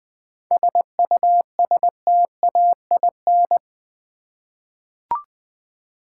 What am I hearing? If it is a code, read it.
Morse code, heard as a keyed tone, about 20 wpm, 700 Hz. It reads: SUSTAIN
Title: 7 Letter Words - First 3 Letters Given - 20wpm